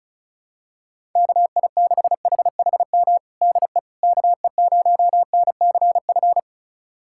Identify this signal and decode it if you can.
Morse code, heard as a keyed tone, about 36 wpm, 700 Hz. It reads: KI6HHM DE KE0NCF
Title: USA Callsign DE Callsign 35wpm